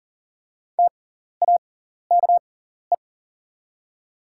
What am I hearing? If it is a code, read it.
Morse code, heard as a keyed tone, about 39 wpm, spaced as for 14 wpm, 690 Hz. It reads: TAKE
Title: 6x Character Spacing - Top 200 Words - 40wpm